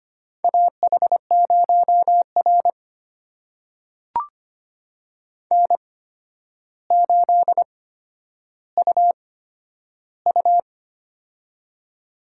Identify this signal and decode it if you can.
Morse code, heard as a keyed tone, about 25 wpm, spaced as for 11 wpm, 700 Hz. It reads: AH0R N8UU
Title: USA Call Signs - 4 Characters - 8x Character Spacing - 25wpm